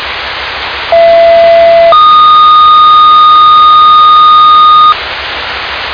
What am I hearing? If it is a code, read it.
Two-tone page, heard as a one-second tone, then a three-second tone, about 670 then 1230 Hz, with noise. It is heard from 0.9 to 4.9 s.